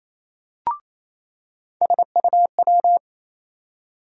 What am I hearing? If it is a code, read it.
Morse code, heard as a keyed tone, about 28 wpm, 700 Hz. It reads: SUW
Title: USA Call Sign Postfixes - 3 Letters 28wpm